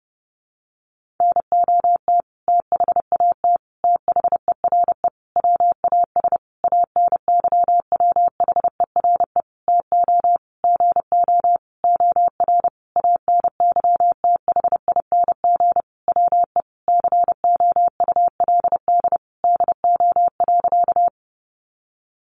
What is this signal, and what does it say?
Morse code, heard as a keyed tone, about 30 wpm, 700 Hz. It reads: NOT THAT THERE WAS ANYWHERE TO GO OR ANYTHING WE COULD DO.
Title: Sentences from Top 1400 Words 30wpm